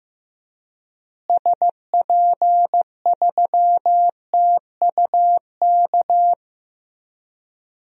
Morse code, heard as a keyed tone, about 15 wpm, 700 Hz. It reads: SP3TUK
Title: DXCC-100 Call Signs - 6 Characters - Repeated 3x - 15wpm